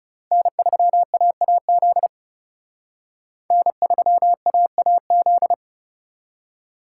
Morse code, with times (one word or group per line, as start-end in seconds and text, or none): 0.3-2.1 N3AAZ
3.5-5.5 N3AAZ